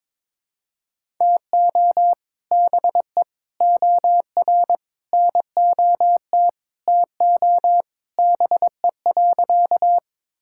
Morse code, heard as a keyed tone, about 22 wpm, 700 Hz. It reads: TO BE OR NOT TO BE.
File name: practice / 22wpm / Sentences with 2-Word Phrases - 22wpm